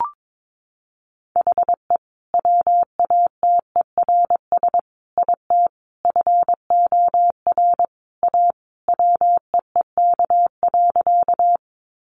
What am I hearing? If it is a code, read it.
Morse code, heard as a keyed tone, about 22 wpm, 700 Hz. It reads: HE WATERS IT FOR A WEEK.
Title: Easy Phrases 22wpm